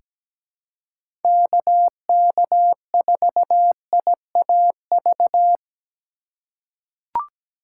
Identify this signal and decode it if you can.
Morse code, heard as a keyed tone, about 17 wpm, 700 Hz. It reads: KK4IAV
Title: USA Call Signs - 6 Characters - 2x Character Spacing - 17wpm